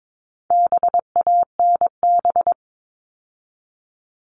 Morse code, heard as a keyed tone, about 22 wpm, 700 Hz. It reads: BANB
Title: All Cut Numbers - 4-Digit 22wpm